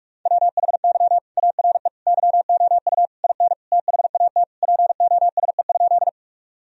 Morse code, heard as a keyed tone, about 41 wpm, 700 Hz. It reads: WHY ARE YOU IN THAT POSE?
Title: Sentences from Top 1000 Words - Review Part 2 - 45wpm